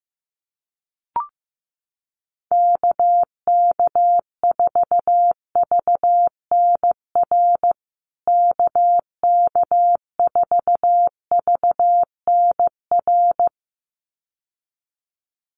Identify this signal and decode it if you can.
Morse code, heard as a keyed tone, about 15 wpm, 700 Hz. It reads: KK4VNR KK4VNR
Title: USA Call Signs Repeated 2x 15wpm